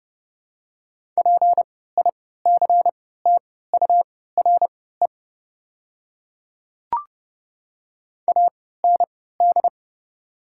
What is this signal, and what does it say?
Morse code, heard as a keyed tone, about 30 wpm, spaced as for 17 wpm, 700 Hz. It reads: PICTURE AND
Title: 3x Character Spacing - Top 300 Words - 30wpm